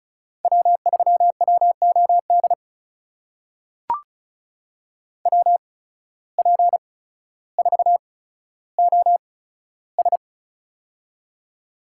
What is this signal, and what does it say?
Morse code, heard as a keyed tone, about 36 wpm, spaced as for 15 wpm, 700 Hz. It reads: W3WOD WP4OS
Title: USA Call Signs - 5 Characters - 8x Character Spacing - 35wpm